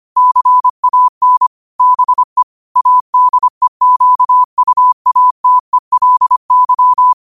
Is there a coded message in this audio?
Yes: Morse code from 0.2 to 7.1 s, heard as a keyed tone, about 25 wpm, 1000 Hz. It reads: CAN BE ADEQUATELY